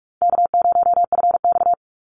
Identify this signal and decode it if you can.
Morse code, heard as a keyed tone, about 44 wpm, 700 Hz. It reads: K0FX
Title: Call Signs - 45wpm